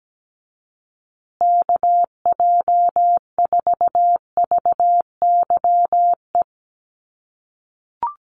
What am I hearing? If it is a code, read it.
Morse code, heard as a keyed tone, about 17 wpm, 700 Hz. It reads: KJ4VYE
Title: USA Call Signs Repeated 2x 17wpm